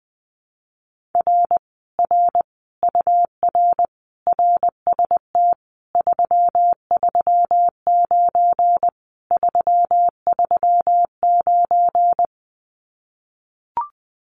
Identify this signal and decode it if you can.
Morse code, heard as a keyed tone, about 20 wpm, 700 Hz. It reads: R R UR RST 339 339
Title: Common QSO Phrases Encore 20wpm